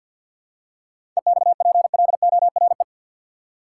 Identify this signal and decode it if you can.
Morse code, heard as a keyed tone, about 50 wpm, 700 Hz. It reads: EXPLORE
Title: Seven Letter Words - 50wpm